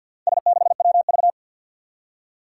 Morse code, heard as a keyed tone, about 52 wpm, 700 Hz. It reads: S6WV
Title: Four Challenging Letter-Number Combinations 50wpm